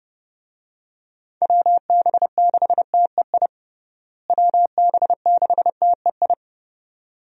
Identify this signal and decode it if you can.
Morse code, heard as a keyed tone, about 30 wpm, 700 Hz. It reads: WB6TEI WB6TEI
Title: USA Call Signs - 6 Characters - Repeated 2x - 30wpm